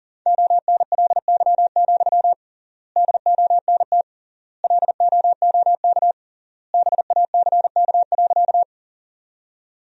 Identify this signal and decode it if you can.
Morse code, heard as a keyed tone, about 40 wpm, 700 Hz. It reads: ONLY, DONT LOOK BACK.
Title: 3x Word-Spacing - Sentences from Top 300 Words - 40wpm